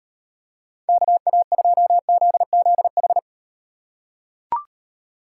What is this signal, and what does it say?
Morse code, heard as a keyed tone, about 38 wpm, 700 Hz. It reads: KA2ZZH